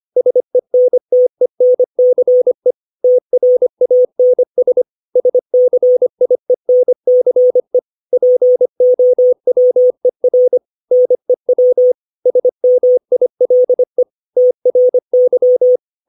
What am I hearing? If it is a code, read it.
Morse code, heard as a keyed tone, about 25 wpm, 500 Hz. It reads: SENTENCE TRANS SCIENCE POWER NEW SMILE TRY
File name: 632 words / 632 Words 25WPM